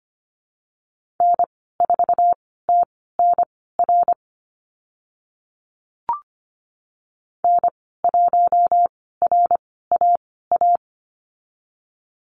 Morse code, heard as a keyed tone, about 25 wpm, spaced as for 16 wpm, 690 Hz. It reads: N4TNR N1RAA
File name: Lesson 15g - Number 5 - Call Signs 25wpm